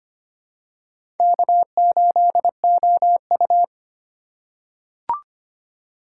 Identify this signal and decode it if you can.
Morse code, heard as a keyed tone, about 25 wpm, 700 Hz. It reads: K8OU
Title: USA Call Signs - 4 Characters - 4x Character Spacing - 25wpm